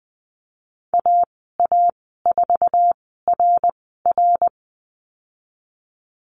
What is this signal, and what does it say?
Morse code, heard as a keyed tone, about 20 wpm, spaced as for 15 wpm, 700 Hz. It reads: AA4RR
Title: 069 Lesson 12g - Letter d - Call Signs 20wpm